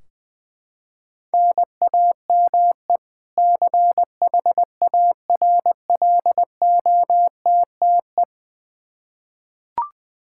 Morse code, heard as a keyed tone, about 20 wpm, 700 Hz. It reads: NAME CHARLOTTE
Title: Practice QSO - Exchange 1 - Beginning - 20wpm